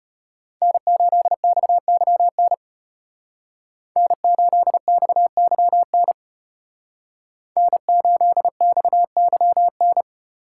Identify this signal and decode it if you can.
Morse code, heard as a keyed tone, about 33 wpm, 700 Hz. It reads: N8XYN N8XYN N8XYN